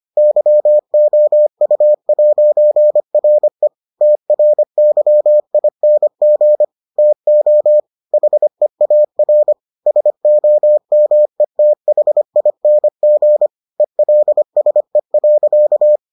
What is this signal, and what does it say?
Morse code, heard as a keyed tone, about 25 wpm, 600 Hz. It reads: YOU'RE TRYING TO HEAR SOMETHING ELSE.